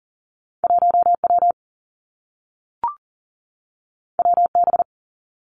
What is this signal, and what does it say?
Morse code, heard as a keyed tone, about 42 wpm, 700 Hz. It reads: 1 W W B
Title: Two Challenging Letter-Number Combinations 40wpm